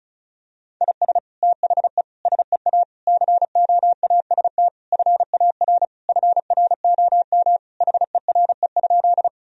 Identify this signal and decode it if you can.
Morse code, heard as a keyed tone, about 34 wpm, 700 Hz. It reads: IS THE SEA COAST FAR FROM HERE?
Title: Sentences from Top 1000 Words - Review Part 4 - 35wpm